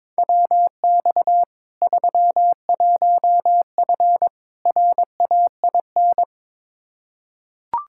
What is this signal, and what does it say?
Morse code, heard as a keyed tone, about 22 wpm, 700 Hz. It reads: WX 31F RAIN
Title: Practice QSO - Exchange 2 - Info - 22wpm